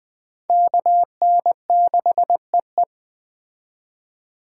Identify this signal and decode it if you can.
Morse code, heard as a keyed tone, about 20 wpm, 700 Hz. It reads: KN6EE